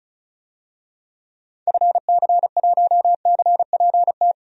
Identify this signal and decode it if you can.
Morse code, heard as a keyed tone, about 34 wpm, 700 Hz. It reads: FC1CPT
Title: DXCC-100 Call Signs - 6 Characters - 35wpm